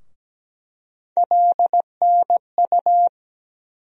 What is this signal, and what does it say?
Morse code, heard as a keyed tone, about 17 wpm, 700 Hz. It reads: LNU
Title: USA Call Sign Postfixes - 3 Letters 17wpm